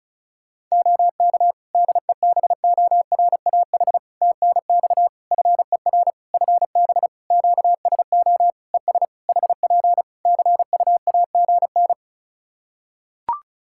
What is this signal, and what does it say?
Morse code, heard as a keyed tone, about 36 wpm, 700 Hz. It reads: OK DEBORAH TNX FER FB QSO ES HP CUAGN
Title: Practice QSO - Exchange 4 - Ending - 35wpm